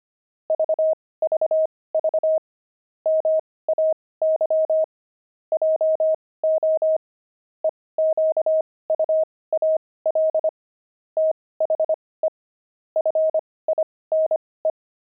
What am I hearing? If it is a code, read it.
Morse code, heard as a keyed tone, about 25 wpm, spaced as for 18 wpm, 630 Hz. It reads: VVV MAY JO EQUAL THE FINE
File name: pangram-fs-18-cs-25-puzzles